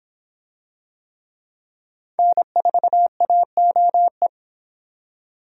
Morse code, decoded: N4AOE

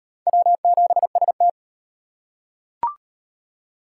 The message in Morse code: W7ST

